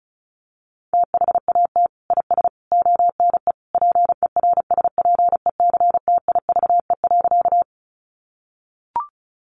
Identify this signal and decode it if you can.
Morse code, heard as a keyed tone, about 34 wpm, 700 Hz. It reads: THAT IS ONE PERSPECTIVE.
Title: Sentences from Top 1200 Words 35wpm